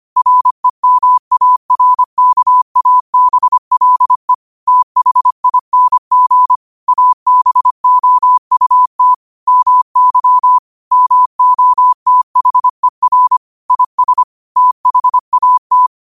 Morse code, heard as a keyed tone, about 25 wpm, 1000 Hz. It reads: REMARKABLE THING ABOUT MY MOTHER IS THAT
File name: qotdfiles / 25wpm